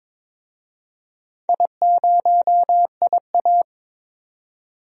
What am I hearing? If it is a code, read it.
Morse code, heard as a keyed tone, about 22 wpm, 700 Hz. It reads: I0IA
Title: DXCC-100 Call Signs - 4 Characters - 22wpm